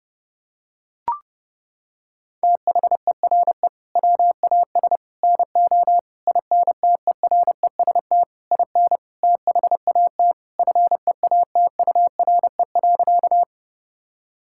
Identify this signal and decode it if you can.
Morse code, heard as a keyed tone, about 30 wpm, 700 Hz. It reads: THERE WAS NO INTEREST IN THAT FEATURE.
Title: Sentences from Top 1000 Words - Review Part 5 - 30wpm